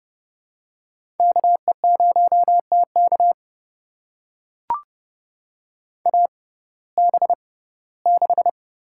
Morse code, heard as a keyed tone, about 30 wpm, spaced as for 18 wpm, 700 Hz. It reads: KE0TK AB6
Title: USA Call Signs - 5 Characters - 6x Character Spacing - 30wpm